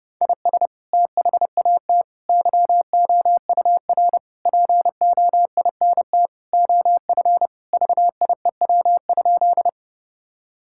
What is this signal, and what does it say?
Morse code, heard as a keyed tone, about 32 wpm, 700 Hz. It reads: IS THAT YOUR POINT OF VIEW?
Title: Sentences from Top 1000 Words - Review Part 6 - 30wpm